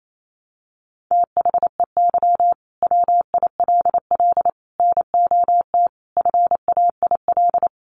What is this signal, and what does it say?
Morse code, heard as a keyed tone, about 28 wpm, 700 Hz. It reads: THEY WILL NOT FAIL
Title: Sentences from Top 1100 Words 28wpm